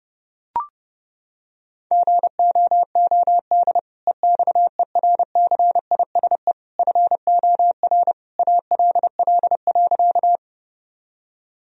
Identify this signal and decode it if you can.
Morse code, heard as a keyed tone, about 30 wpm, 700 Hz. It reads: GOOD EXERCISE FOR ALL.
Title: Sentences from Top 1000 Words - Review Part 4 - 30wpm